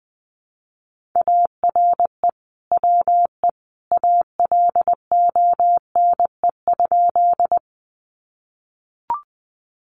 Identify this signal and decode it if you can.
Morse code, heard as a keyed tone, about 20 wpm, 700 Hz. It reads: ARE WE ALONE?
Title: Sentences from Top 1000 Words 20wpm